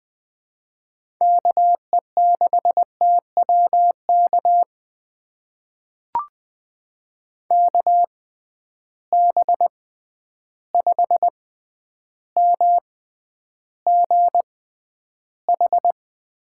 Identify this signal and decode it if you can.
Morse code, heard as a keyed tone, about 20 wpm, spaced as for 10 wpm, 700 Hz. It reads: KE6TWK KB5MGH
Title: USA Call Signs - 6 Characters - 6x Character Spacing - 20wpm